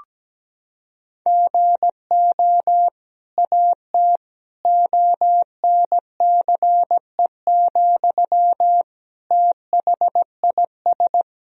Morse code, heard as a keyed tone, about 17 wpm, 700 Hz. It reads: GO AT ONCE, THIS
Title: Sentences from Top 1000 Words - Review Part 1 - 17wpm